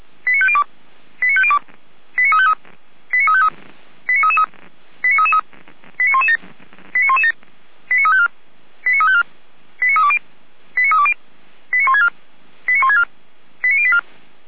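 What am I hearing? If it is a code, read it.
Five-tone selective call (ZVEI), heard as quick runs of five tones, tones of about 70 ms. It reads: 89502 89502 89353 89353 89333 89333 89117 89117 89354 89354 89320 89320 89164 89164 89084
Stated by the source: ZVEI five-tone selective call received via SDR.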